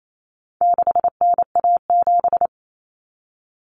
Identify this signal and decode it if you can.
Morse code, heard as a keyed tone, about 28 wpm, 700 Hz. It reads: 6NA7
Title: Common Cut Numbers - 4-Digit 28wpm